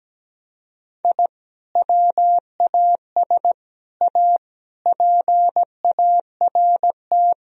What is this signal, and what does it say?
Morse code, heard as a keyed tone, about 17 wpm, 700 Hz. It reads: I WAS A PART
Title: Sentences from Top 1000 Words - Review Part 1 - 17wpm